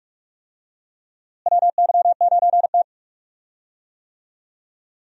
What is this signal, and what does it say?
Morse code, heard as a keyed tone, about 46 wpm, 700 Hz. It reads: WY9T
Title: USA Call Signs - 4 Characters - Repeated 3x - 45wpm